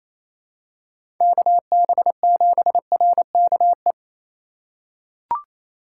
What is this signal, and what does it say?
Morse code, heard as a keyed tone, about 28 wpm, 700 Hz. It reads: KB7RKE
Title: USA Call Signs - 8x Character Spacing - 28wpm